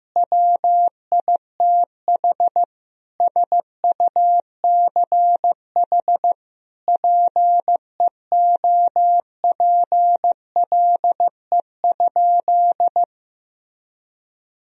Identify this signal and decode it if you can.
Morse code, heard as a keyed tone, about 15 wpm, 700 Hz. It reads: WITH SUCH PEOPLE?
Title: Sentences from Top 1000 Words - Review Part 3 - 15wpm